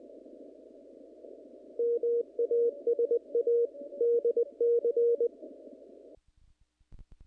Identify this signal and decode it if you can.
Morse code, heard as a keyed tone, about 20 wpm, 470 Hz. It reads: MASA DC
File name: contests / naqp-cw-k5tr-op-k5pi- -1900